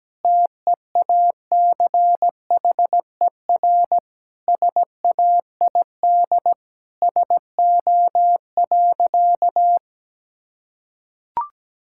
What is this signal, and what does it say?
Morse code, heard as a keyed tone, about 17 wpm, 700 Hz. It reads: TEACHER SAID SO.
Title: Sentences from Top 1000 Words - Review Part 6 - 17wpm